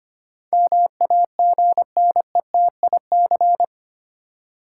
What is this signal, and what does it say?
Morse code, heard as a keyed tone, about 25 wpm, 700 Hz. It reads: MAGNETIC
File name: Eight Letter Words - 25wpm